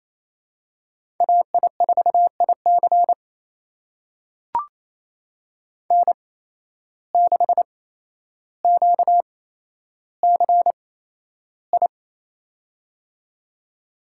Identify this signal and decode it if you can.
Morse code, heard as a keyed tone, about 28 wpm, spaced as for 12 wpm, 700 Hz. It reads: AI4IC N6QCI
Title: USA Call Signs - 5 Characters - 8x Character Spacing - 28wpm